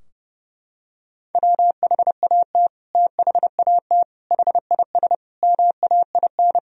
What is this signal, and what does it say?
Morse code, heard as a keyed tone, about 30 wpm, 700 Hz. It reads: WHAT THAT HIS MAIN
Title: Sentences from Top 1000 Words - Review Part 2 - 30wpm